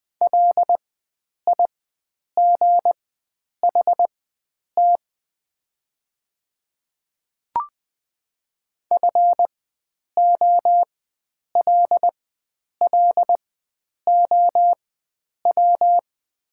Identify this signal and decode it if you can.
Morse code, heard as a keyed tone, about 20 wpm, spaced as for 9 wpm, 700 Hz. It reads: LIGHT FOLLOW
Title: 4x Character Spacing - Top 300 Words - 20wpm